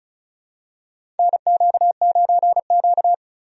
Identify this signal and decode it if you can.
Morse code, heard as a keyed tone, about 36 wpm, 700 Hz. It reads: NQ9Q
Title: USA Call Signs - 4 Characters - 35wpm